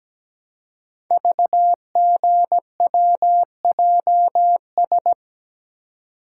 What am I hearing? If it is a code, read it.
Morse code, heard as a keyed tone, about 17 wpm, 700 Hz. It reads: VGWJS
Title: Code Groups - 5 Letters - 17wpm